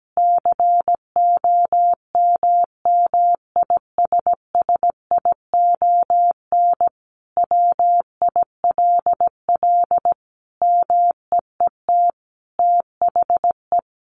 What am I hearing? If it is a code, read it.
Morse code, heard as a keyed tone, about 17 wpm, 700 Hz. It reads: COMMISSION WILL MEET THE
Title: Sentences from Top 1400 Words 17wpm